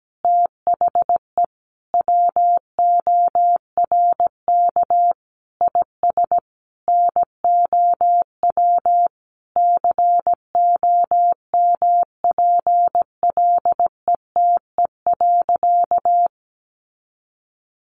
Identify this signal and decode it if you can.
Morse code, heard as a keyed tone, about 17 wpm, 700 Hz. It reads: THE WORK IS NOW COMPLETE.
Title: Sentences from Top 400 Words 17wpm